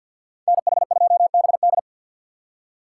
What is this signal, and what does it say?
Morse code, heard as a keyed tone, about 48 wpm, 700 Hz. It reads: NH2BD
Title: USA Call Signs - 5 Characters - 2x Character Spacing - 50wpm